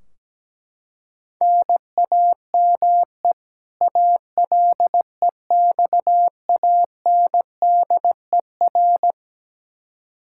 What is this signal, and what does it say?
Morse code, heard as a keyed tone, about 17 wpm, 700 Hz. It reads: NAME ALEXANDER
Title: Practice QSO - Exchange 1 - Beginning - 17wpm